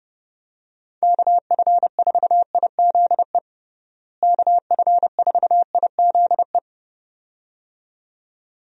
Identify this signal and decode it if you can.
Morse code, heard as a keyed tone, about 32 wpm, 700 Hz. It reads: KF4IZE KF4IZE
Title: USA Call Signs - 6 Characters - Repeated 2x - 30wpm